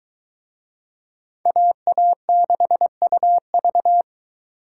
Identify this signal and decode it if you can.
Morse code, heard as a keyed tone, about 21 wpm, 700 Hz. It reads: AA6UV